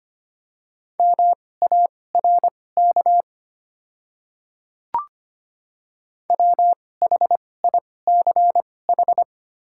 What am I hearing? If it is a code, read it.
Morse code, heard as a keyed tone, about 25 wpm, spaced as for 18 wpm, 700 Hz. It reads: MARK WHICH
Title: 2x Character Spacing - Top 300 Words - 25wpm